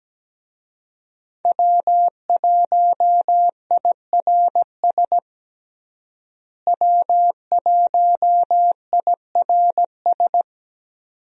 Morse code, heard as a keyed tone, about 17 wpm, 700 Hz. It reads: W1IRS W1IRS
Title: USA Call Signs - 5 Characters - Repeated 2x - 17wpm